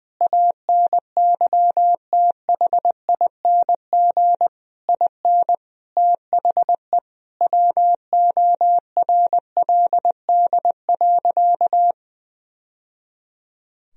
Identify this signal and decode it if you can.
Morse code, heard as a keyed tone, about 21 wpm, 700 Hz. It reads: ANYTHING IN THE WORLD.
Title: Sentences from Top 1000 Words - Review Part 3 - 20wpm